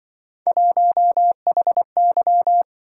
Morse code, heard as a keyed tone, about 25 wpm, 700 Hz. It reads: 1HY